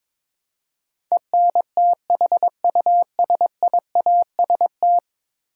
Morse code, heard as a keyed tone, about 22 wpm, 700 Hz. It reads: ENTHUSIAST
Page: morse-code-ninja-practice-22wpm